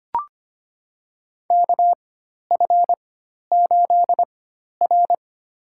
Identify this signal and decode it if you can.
Morse code, heard as a keyed tone, about 25 wpm, spaced as for 12 wpm, 700 Hz. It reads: KF8R
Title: USA Call Signs - 4x Character Spacing - 25wpm